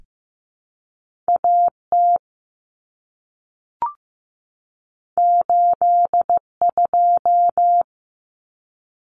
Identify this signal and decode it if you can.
Morse code, heard as a keyed tone, about 15 wpm, 700 Hz. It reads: AT 82
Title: Common Cut Numbers - 2-Digit 15wpm